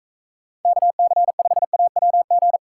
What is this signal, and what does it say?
Morse code, heard as a keyed tone, about 41 wpm, 700 Hz. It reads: KC5AWG